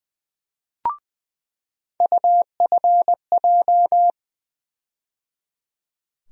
Morse code, heard as a keyed tone, about 20 wpm, 700 Hz. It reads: UFJ
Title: USA Call Sign Postfixes - 3 Letters 20wpm